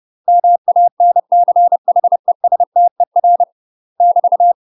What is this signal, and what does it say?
Morse code, heard as a keyed tone, about 30 wpm, 700 Hz. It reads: MANCHESTER <BT>
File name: UK MASTER QSOs TRAINING / 30WPM / UK MASTER QSOs 0016